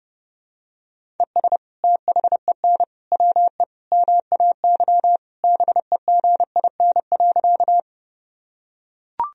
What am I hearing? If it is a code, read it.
Morse code, heard as a keyed tone, about 30 wpm, 700 Hz. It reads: ES THEN WE MAY BEGIN.
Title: Sentences from Top 300 Words and Abbreviations 30wpm